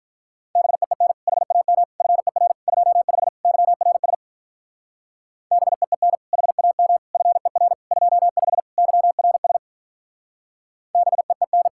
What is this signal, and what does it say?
Morse code, heard as a keyed tone, about 43 wpm, 700 Hz. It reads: BEEN HAM FER 25 YRS BEEN HAM FER 25 YRS BEEN
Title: Speed-Racing Common QSO Phrases Encore - 35wpm